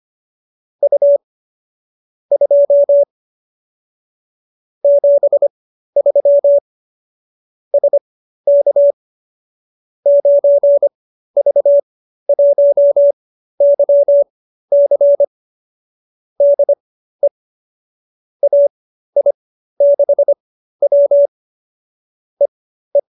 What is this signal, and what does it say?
Morse code, heard as a keyed tone, about 25 wpm, spaced as for 13 wpm, 570 Hz. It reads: U 2 73 SK 9V1YC DE AI6W EE